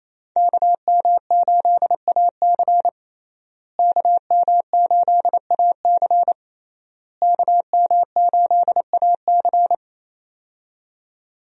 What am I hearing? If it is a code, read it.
Morse code, heard as a keyed tone, about 28 wpm, 700 Hz. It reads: KM8AC KM8AC KM8AC
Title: USA Call Signs - 5 Characters - Repeated 3x - 28wpm